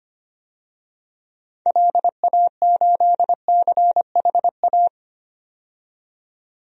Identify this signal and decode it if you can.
Morse code, heard as a keyed tone, about 25 wpm, 700 Hz. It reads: LA8CHA
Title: DXCC-100 Call Signs - 6 Characters - 25wpm